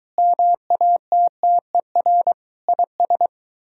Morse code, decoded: MATTER IS

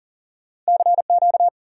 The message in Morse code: CQ

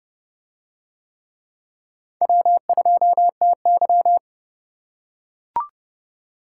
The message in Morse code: W2TY

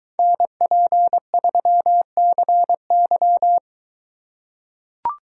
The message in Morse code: NP3CY

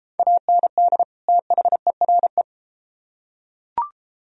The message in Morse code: AND THERE

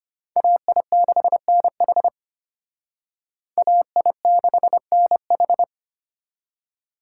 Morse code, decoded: AI6NH AI6NH

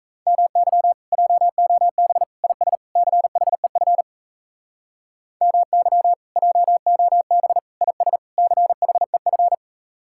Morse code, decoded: MY JOB IS CHEF MY JOB IS CHEF

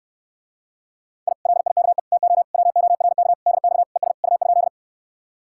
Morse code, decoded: I LIKE TO WORK DX ES U?

